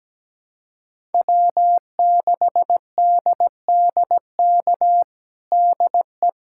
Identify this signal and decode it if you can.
Morse code, heard as a keyed tone, about 17 wpm, 700 Hz. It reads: W6DDK DE
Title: USA Callsign DE Callsign - 5 Characters - 17wpm